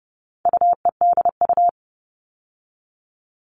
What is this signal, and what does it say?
Morse code, heard as a keyed tone, about 30 wpm, 700 Hz. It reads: UEDU
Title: All Cut Numbers - 4-Digit 30wpm